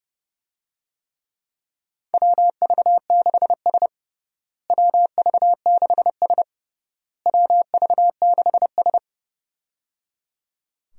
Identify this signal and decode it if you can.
Morse code, heard as a keyed tone, about 29 wpm, 700 Hz. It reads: WV6S WV6S WV6S
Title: USA Call Signs - 4 Characters - Repeated 3x - 30wpm